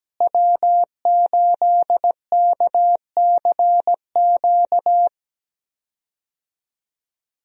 Morse code, heard as a keyed tone, about 17 wpm, 700 Hz. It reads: W8KCQ
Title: USA Call Signs - 5 Characters - Repeated 2x - 17wpm